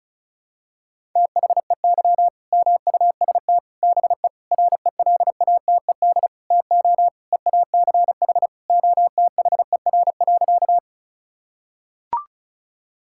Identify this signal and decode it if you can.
Morse code, heard as a keyed tone, about 34 wpm, 700 Hz. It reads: THEY MUST BE RELATED TO EACH OTHER.
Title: Sentences from Top 1000 Words - Review Part 5 - 35wpm